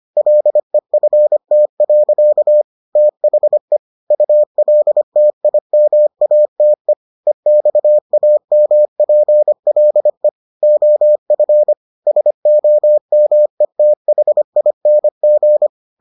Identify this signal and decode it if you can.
Morse code, heard as a keyed tone, about 25 wpm, 600 Hz. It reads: LEFT. THE ULTIMATE EXAMPLE OF SOMETHING